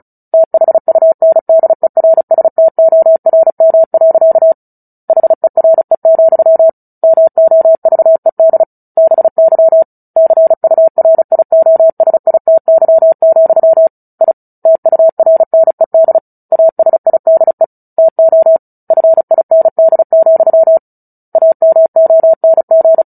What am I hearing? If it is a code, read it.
Morse code, heard as a keyed tone, about 34 wpm, 650 Hz. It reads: THUNDERSTORM. HERE, MOVED BY CURIOSITY, I TURNED ASIDE TO FIND, AMONG
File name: Book Two / 09-Chapter 9 - 35 WPM